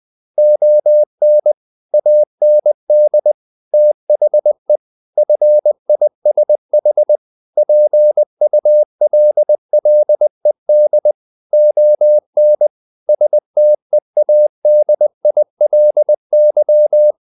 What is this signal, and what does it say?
Morse code, heard as a keyed tone, about 20 wpm, 600 Hz. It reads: ON AND THE FISH PULLED ON STEADILY
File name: the old man and the sea part 2 20wpm